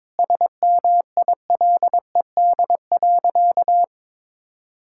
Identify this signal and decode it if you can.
Morse code, heard as a keyed tone, about 22 wpm, 700 Hz. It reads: SMILED.
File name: practice / 22wpm / Sentences with 4-Word Phrases - 22wpm